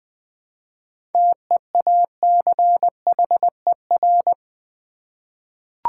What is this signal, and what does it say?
Morse code, heard as a keyed tone, about 21 wpm, 700 Hz. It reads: TEACHER
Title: Seven Letter Words - 20wpm